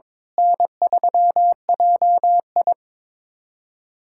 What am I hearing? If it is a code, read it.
Morse code, heard as a keyed tone, about 21 wpm, 700 Hz. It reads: N3JI
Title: USA Callsign DE Callsign - 4 Characters - 22wpm